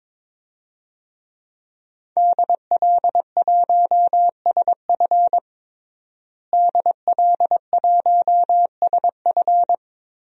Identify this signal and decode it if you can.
Morse code, heard as a keyed tone, about 21 wpm, 700 Hz. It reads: DL1SF DL1SF
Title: DXCC-100 Call Signs - 5 Characters - Repeated 2x - 22wpm